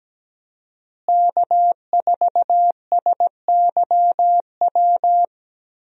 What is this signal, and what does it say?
Morse code, heard as a keyed tone, about 17 wpm, 700 Hz. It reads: K4SYW
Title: USA Call Signs - 5 Characters - Repeated 3x - 17wpm